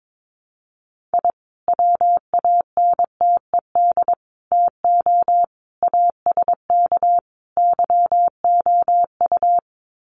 Morse code, heard as a keyed tone, about 22 wpm, 700 Hz. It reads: I WANTED TO ASK YOU
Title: Sets of 5 Words Encore2 22wpm